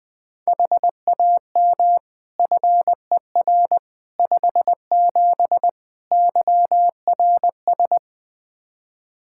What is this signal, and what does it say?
Morse code, heard as a keyed tone, about 21 wpm, 700 Hz. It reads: HAM FER 57 YRS
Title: Practice QSO - Exchange 3 - More Info - 20wpm